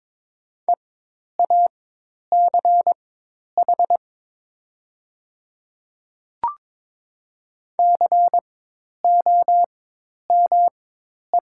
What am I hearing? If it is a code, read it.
Morse code, heard as a keyed tone, about 22 wpm, spaced as for 10 wpm, 700 Hz. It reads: EACH COME
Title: 4x Character Spacing - Top 100 Words - 22wpm